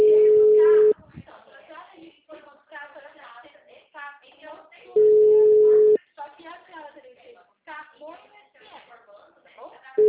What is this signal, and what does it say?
Ringback tone (Europe, 425 Hz), with speech.